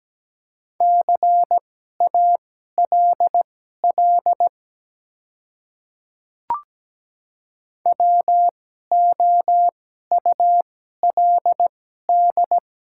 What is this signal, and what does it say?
Morse code, heard as a keyed tone, about 17 wpm, spaced as for 12 wpm, 700 Hz. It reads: CALL WOULD
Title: 2x Character Spacing - Top 200 Words - 17wpm